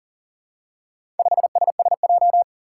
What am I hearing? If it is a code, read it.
Morse code, heard as a keyed tone, about 42 wpm, 700 Hz. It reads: 5SSJ
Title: Four Challenging Letter-Number Combinations 40wpm